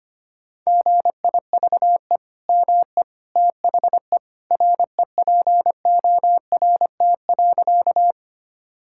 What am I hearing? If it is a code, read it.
Morse code, heard as a keyed tone, about 25 wpm, 700 Hz. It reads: GIVE ME THE REPORT.
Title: Sentences from Top 1000 Words - Review Part 6 - 25wpm